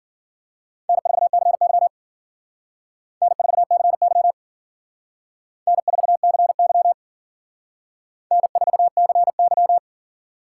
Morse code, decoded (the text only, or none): N4CY N4CY N4CY N4CY